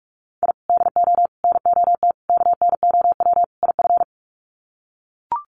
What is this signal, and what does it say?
Morse code, heard as a keyed tone, about 44 wpm, 700 Hz. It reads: I DO NOT KNOW IF
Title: Sets of 5 Words 45wpm